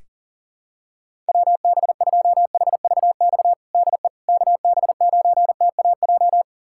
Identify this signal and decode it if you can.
Morse code, heard as a keyed tone, about 40 wpm, 700 Hz. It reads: WB2HVX DE KB9TAJ
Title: USA Callsign DE Callsign - 6 Characters - 40wpm